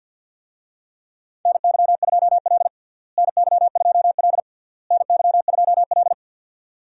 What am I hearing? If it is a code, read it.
Morse code, heard as a keyed tone, about 49 wpm, 700 Hz. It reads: NY2L NY2L NY2L
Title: USA Call Signs - 4 Characters - Repeated 3x - 50wpm